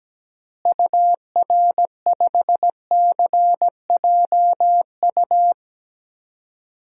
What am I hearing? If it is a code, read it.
Morse code, heard as a keyed tone, about 17 wpm, 700 Hz. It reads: UR5CJU
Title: DXCC-100 Call Signs - 6 Characters - Repeated 2x - 17wpm